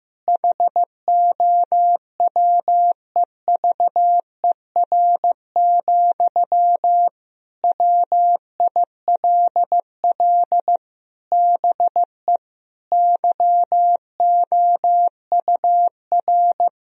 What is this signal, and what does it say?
Morse code, heard as a keyed tone, about 16 wpm, 700 Hz. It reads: HOWEVER, WILL BE YOUR
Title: Sentences from Top 1000 Words - Review Part 4 - 15wpm